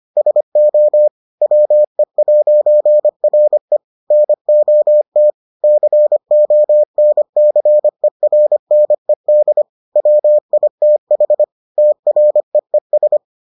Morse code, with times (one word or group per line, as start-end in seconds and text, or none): 0.2-1.1 SO
1.4-3.8 WE'RE
4.1-5.3 NOT
5.6-9.6 CONCERNED
10.0-11.4 WITH
11.8-13.2 TREES